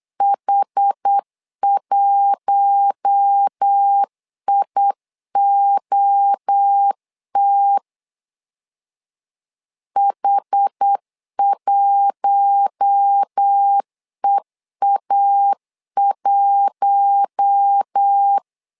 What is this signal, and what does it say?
Morse code, heard as a keyed tone, about 8 wpm, 800 Hz. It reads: H1IOT H1EA1
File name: LEZIONI / 4 Q1 8wpm 3035